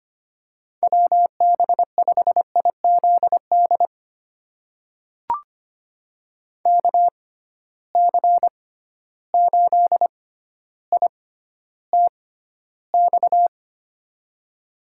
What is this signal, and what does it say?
Morse code, heard as a keyed tone, about 25 wpm, spaced as for 13 wpm, 700 Hz. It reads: WB5IZD KC8ITX
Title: USA Call Signs - 6x Character Spacing - 25wpm